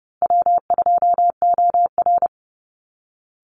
Morse code, heard as a keyed tone, about 30 wpm, 700 Hz. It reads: W2OR
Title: Easy Call Signs Repeated 3x - 30wpm